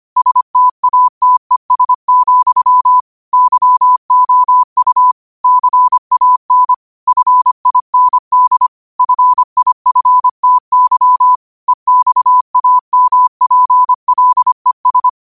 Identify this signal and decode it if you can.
Morse code, heard as a keyed tone, about 25 wpm, 1000 Hz. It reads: STATES, YOU CAN FIND FIFTY EXAMPLES
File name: qotdfiles / 25wpm